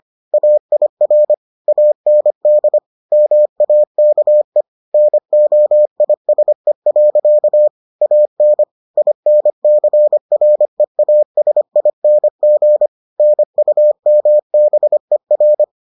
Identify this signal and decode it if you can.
Morse code, heard as a keyed tone, about 25 wpm, 600 Hz. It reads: AIR AND MAKE NOISE. AN INCREASING NUMBER